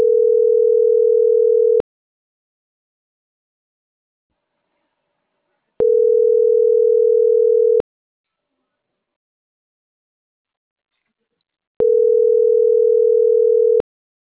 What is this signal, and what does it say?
Ringback tone (North America), 440 + 480 Hz.